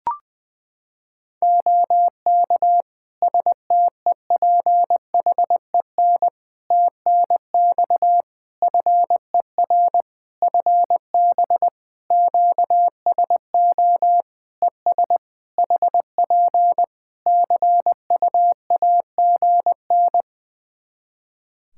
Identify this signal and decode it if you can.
Morse code, heard as a keyed tone, about 21 wpm, 700 Hz. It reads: OK STEPHEN TNX FER FB QSO ES HP CUAGN
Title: K1USN SST QSO Practice 20wpm